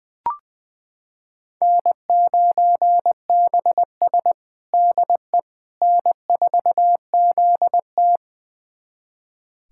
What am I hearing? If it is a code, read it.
Morse code, heard as a keyed tone, about 20 wpm, 700 Hz. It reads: N9BS DE N4ZT
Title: USA Callsign DE Callsign - 4 Characters - 20wpm